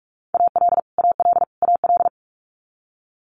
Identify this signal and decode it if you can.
Morse code, heard as a keyed tone, about 46 wpm, 700 Hz. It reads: AL AL AL
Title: Multi-letter Phonemes Repeated 3x 45wpm